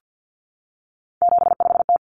Morse code, heard as a keyed tone, about 50 wpm, 700 Hz. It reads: 75T
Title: Common Cut Numbers - 3-Digit 50wpm